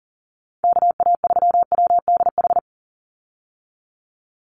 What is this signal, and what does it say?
Morse code, heard as a keyed tone, about 40 wpm, 700 Hz. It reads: KA3WDH
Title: USA Call Signs 40wpm